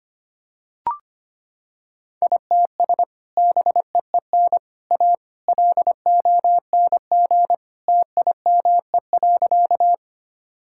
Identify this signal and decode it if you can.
Morse code, heard as a keyed tone, about 25 wpm, 700 Hz. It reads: ITS BEEN A LONG TIME.
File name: practice / 25wpm / Sentences with 3-Word Phrases - 25wpm